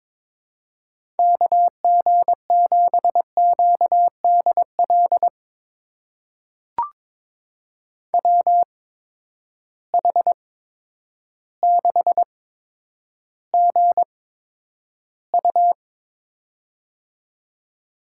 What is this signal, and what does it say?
Morse code, heard as a keyed tone, about 22 wpm, spaced as for 10 wpm, 700 Hz. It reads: KG7QDL WH6GU
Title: USA Call Signs - 8x Character Spacing - 22wpm